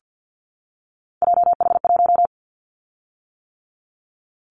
Morse code, heard as a keyed tone, about 50 wpm, 700 Hz. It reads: JH1
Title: Callsign Prefixes for Top 25 DXCC - 50wpm